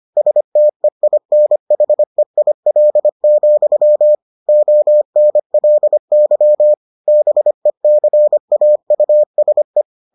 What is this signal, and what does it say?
Morse code, heard as a keyed tone, about 25 wpm, 600 Hz. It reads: STEINHEIL, ONLY BECAUSE